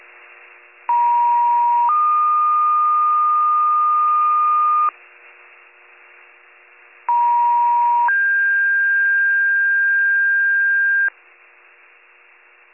Two-tone page, heard as a one-second tone, then a three-second tone, about 950 then 1220 Hz, about 950 then 1600 Hz. It is heard from 0.9 to 11.1 s.